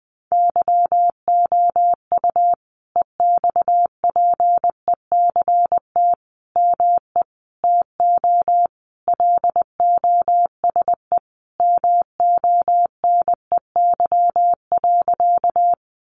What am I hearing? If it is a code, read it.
Morse code, heard as a keyed tone, about 20 wpm, 700 Hz. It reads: YOU EXPECT ME TO LOSE MONEY.
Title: Sentences from Top 1100 Words 20wpm